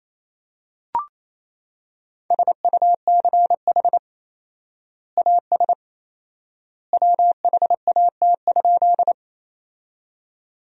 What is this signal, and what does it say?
Morse code, heard as a keyed tone, about 28 wpm, 700 Hz. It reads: SUCH AS WHAT?
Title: 4x Word-Spacing - Sentences from Top 200 Words - 28wpm